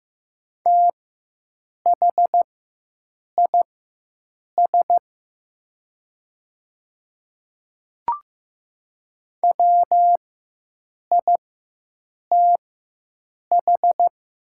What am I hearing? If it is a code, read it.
Morse code, heard as a keyed tone, about 15 wpm, spaced as for 7 wpm, 700 Hz. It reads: THIS WITH
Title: 4x Character Spacing - Top 100 Words - 15wpm